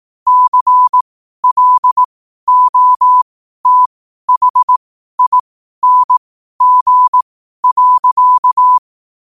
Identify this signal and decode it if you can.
Morse code, heard as a keyed tone, about 18 wpm, spaced as for 12 wpm, 1000 Hz. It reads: CLOTHING.
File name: qotdfiles / 12wpm